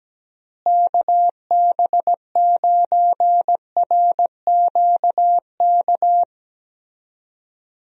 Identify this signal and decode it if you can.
Morse code, heard as a keyed tone, about 17 wpm, 700 Hz. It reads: KB9RQK
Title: USA Call Signs - 6 Characters - Repeated 2x - 17wpm